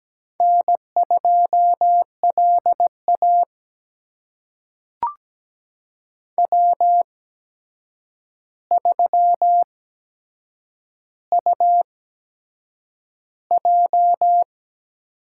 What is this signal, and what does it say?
Morse code, heard as a keyed tone, about 17 wpm, spaced as for 7 wpm, 700 Hz. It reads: N2LA W3UJ
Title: USA Call Signs - 4 Characters - 8x Character Spacing - 17wpm